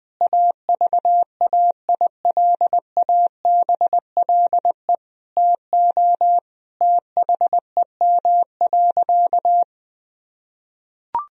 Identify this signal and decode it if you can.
Morse code, heard as a keyed tone, about 20 wpm, 700 Hz. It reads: AVAILABLE TO THEM.
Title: Sentences from Top 1000 Words - Review Part 3 - 20wpm